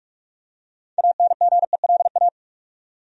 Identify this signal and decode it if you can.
Morse code, heard as a keyed tone, about 42 wpm, 700 Hz. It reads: ANGELA
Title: USA Common Names - 45wpm